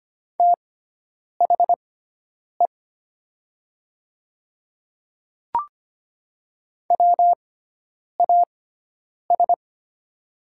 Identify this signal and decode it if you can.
Morse code, heard as a keyed tone, about 25 wpm, spaced as for 9 wpm, 700 Hz. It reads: THE WAS
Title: 6x Character Spacing - Top 100 Words - 25wpm